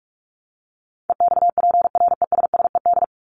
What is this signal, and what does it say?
Morse code, heard as a keyed tone, about 44 wpm, 700 Hz. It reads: EXPRESSED
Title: ED Suffix - Top 100 Words with Base Words 45wpm